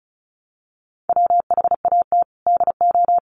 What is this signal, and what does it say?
Morse code, heard as a keyed tone, about 34 wpm, 700 Hz. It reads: WHAT DO
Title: Sentences from Top 300 Words 35wpm